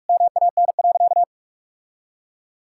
Morse code, heard as a keyed tone, about 46 wpm, 700 Hz. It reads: MAN.
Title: Sentences from Top 1000 Words - Review Part 5 - 45wpm